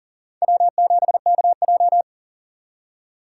Morse code, heard as a keyed tone, about 37 wpm, 700 Hz. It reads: W7KJ
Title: USA Call Signs - 2x Character Spacing - 40wpm